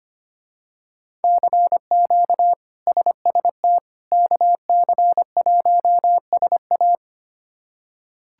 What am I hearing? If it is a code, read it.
Morse code, heard as a keyed tone, about 25 wpm, 700 Hz. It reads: CQ SST KC1SA
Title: K1USN SST QSO Practice - 25wpm